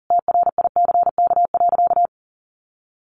Morse code, heard as a keyed tone, about 40 wpm, 700 Hz. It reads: TRICK.
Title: Sentences from Top 1500 Words 40wpm